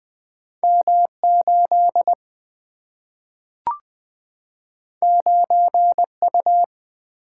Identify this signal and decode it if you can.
Morse code, heard as a keyed tone, about 20 wpm, 700 Hz. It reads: M 8 9 U
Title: Single-Letter Single-Number 20wpm